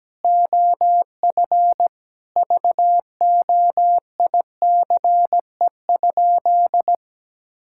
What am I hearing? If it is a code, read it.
Morse code, heard as a keyed tone, about 17 wpm, 700 Hz. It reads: OF VOICE?
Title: Sentences from Top 1600 Words 17wpm